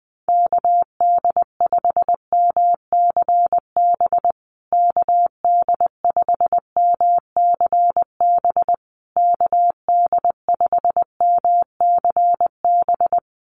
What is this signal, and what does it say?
Morse code, heard as a keyed tone, about 20 wpm, 700 Hz. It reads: KD5MCB KD5MCB KD5MCB
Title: USA Call Signs Repeated 3x 20wpm